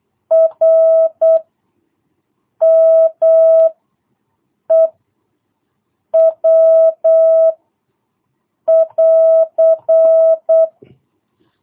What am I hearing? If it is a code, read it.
Morse code, heard as a keyed tone, about 8 wpm, 650 Hz. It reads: R M E W <AR>
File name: jh8cbh / 2022cw